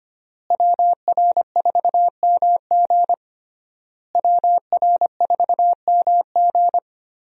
Morse code, heard as a keyed tone, about 25 wpm, 700 Hz. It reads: WR4MG WR4MG
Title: USA Call Signs - 5 Characters - Repeated 3x - 25wpm